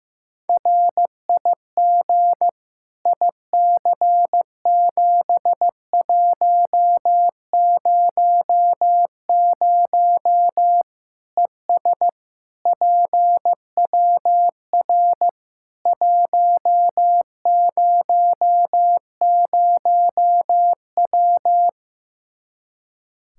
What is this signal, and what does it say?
Morse code, heard as a keyed tone, about 15 wpm, 700 Hz. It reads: RIG IC7100 ES PWR 100W
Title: Practice QSO - Exchange 2 - Info - 15wpm